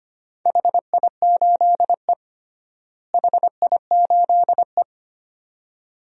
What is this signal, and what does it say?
Morse code, heard as a keyed tone, about 25 wpm, 700 Hz. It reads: HI8E HI8E
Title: DXCC-100 Call Signs - 4 Characters - Repeated 2x - 25wpm